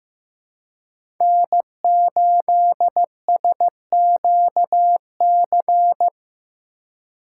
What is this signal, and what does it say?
Morse code, heard as a keyed tone, about 15 wpm, 700 Hz. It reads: N8SQC
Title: USA Call Signs - 4x Character Spacing - 15wpm